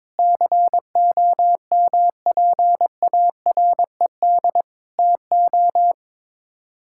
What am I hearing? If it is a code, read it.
Morse code, heard as a keyed tone, about 22 wpm, 700 Hz. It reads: COMPARED TO
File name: practice / 22wpm / Sets of 7 Words - 22wpm